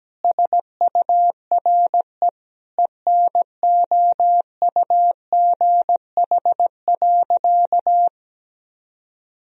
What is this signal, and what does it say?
Morse code, heard as a keyed tone, about 17 wpm, 700 Hz. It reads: SURE ENOUGH.
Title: Sentences from Top 1000 Words - Review Part 1 - 17wpm